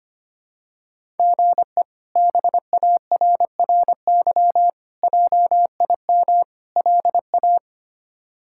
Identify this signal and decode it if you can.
Morse code, heard as a keyed tone, about 25 wpm, 700 Hz. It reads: GE BARRY JIM LA
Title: K1USN SST QSO Practice - 25wpm